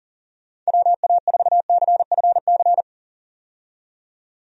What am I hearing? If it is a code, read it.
Morse code, heard as a keyed tone, about 37 wpm, 700 Hz. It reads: WA4CFC
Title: USA Call Signs - 6 Characters - 40wpm